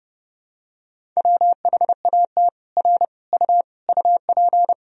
Morse code, heard as a keyed tone, about 30 wpm, 700 Hz. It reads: WHAT R U UP
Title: Sentences from Top 100 Words and Abbreviations 30wpm